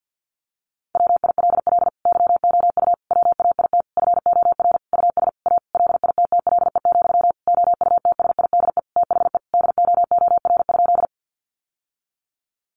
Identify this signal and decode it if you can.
Morse code, heard as a keyed tone, about 48 wpm, 700 Hz. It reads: WILL YOU WAIT FOR US A LITTLE, OUTSIDE THE DOOR?
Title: Sentences from Top 900 Words 50wpm